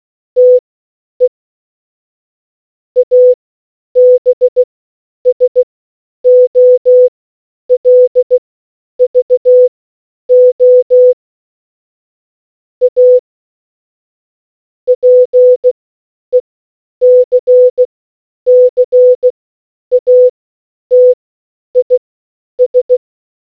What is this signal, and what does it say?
Morse code, heard as a keyed tone, about 16 wpm, spaced as for 9 wpm, 500 Hz. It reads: TE ABSOLVO A PECCATIS